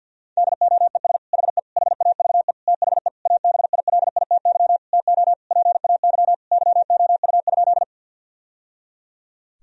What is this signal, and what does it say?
Morse code, heard as a keyed tone, about 53 wpm, 700 Hz. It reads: DOES HE HAVE THE ABILITY TO PAY YOU?
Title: Sentences from Top 1000 Words - Review Part 3 - 50wpm